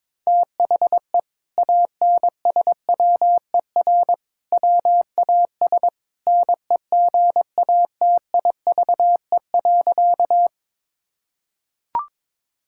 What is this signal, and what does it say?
Morse code, heard as a keyed tone, about 22 wpm, 700 Hz. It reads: THE ANSWER WAS NEGATIVE.
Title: Sentences from Top 1000 Words - Review Part 6 - 22wpm